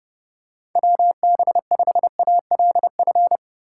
Morse code, heard as a keyed tone, about 30 wpm, 700 Hz. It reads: WB5ALF
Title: USA Call Signs - 6 Characters - 2x Character Spacing - 30wpm